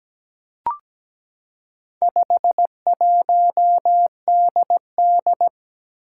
Morse code, heard as a keyed tone, about 17 wpm, 700 Hz. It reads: 51DD
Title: Four Challenging Letter-Number Combinations 17wpm